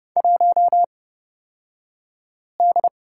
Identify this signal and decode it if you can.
Morse code, heard as a keyed tone, about 30 wpm, spaced as for 5 wpm, 700 Hz. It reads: 1D
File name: Lesson 38d - Letter q - Two Characters 30wpm